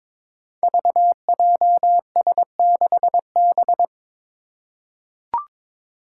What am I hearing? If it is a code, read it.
Morse code, heard as a keyed tone, about 22 wpm, 700 Hz. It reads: VJS6B
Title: Five Challenging Letter-Number Combinations 22wpm